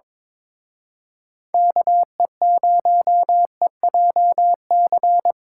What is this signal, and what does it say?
Morse code, heard as a keyed tone, about 26 wpm, 700 Hz. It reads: KE0EJC
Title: Speed Racing - USA Call Signs 22wpm